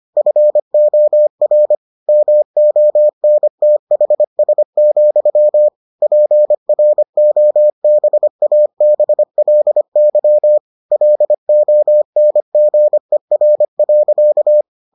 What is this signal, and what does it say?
Morse code, heard as a keyed tone, about 25 wpm, 600 Hz. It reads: FOR MONTHS, PROBABLY LONGER.